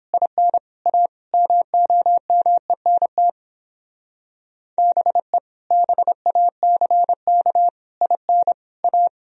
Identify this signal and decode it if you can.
Morse code, heard as a keyed tone, about 29 wpm, 700 Hz. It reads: IN A MOMENT BE BACK IN A